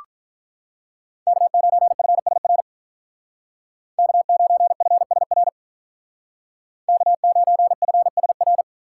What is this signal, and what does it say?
Morse code, heard as a keyed tone, about 46 wpm, 700 Hz. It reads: K9FSR K9FSR K9FSR